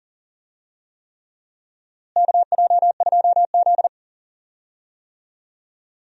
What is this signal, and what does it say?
Morse code, heard as a keyed tone, about 40 wpm, 700 Hz. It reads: KJ2Z
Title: USA Call Signs - 4 Characters - Repeated 3x - 40wpm